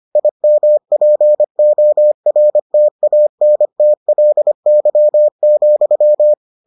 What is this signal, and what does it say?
Morse code, heard as a keyed tone, about 25 wpm, 600 Hz. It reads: IMPORTANTLY,